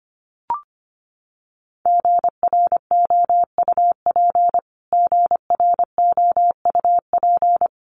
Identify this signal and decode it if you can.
Morse code, heard as a keyed tone, about 25 wpm, 700 Hz. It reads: GROUP GROUP
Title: Top 500 Words Repeated 25wpm